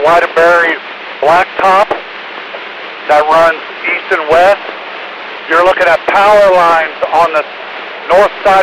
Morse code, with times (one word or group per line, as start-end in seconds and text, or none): none